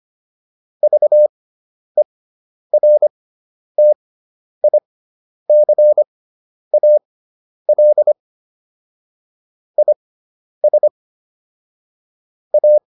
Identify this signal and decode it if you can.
Morse code, heard as a keyed tone, about 25 wpm, spaced as for 10 wpm, 600 Hz. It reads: VERTICAL IS A